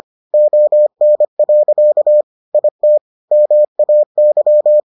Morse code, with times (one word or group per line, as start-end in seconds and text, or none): 0.3-2.2 ON.
2.5-3.0 IT
3.3-4.8 MAY